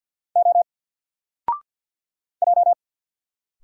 Morse code, heard as a keyed tone, about 50 wpm, 700 Hz. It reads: O J
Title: Single Letter-Number Alt 50wpm